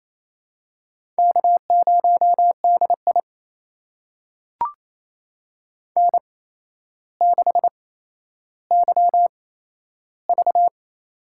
Morse code, heard as a keyed tone, about 28 wpm, spaced as for 12 wpm, 700 Hz. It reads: K0DI N6YV
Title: USA Call Signs - 4 Characters - 8x Character Spacing - 28wpm